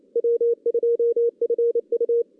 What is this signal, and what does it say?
Morse code, heard as a keyed tone, about 27 wpm, 470 Hz. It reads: W2FU
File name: 2016 JAN CW NAQP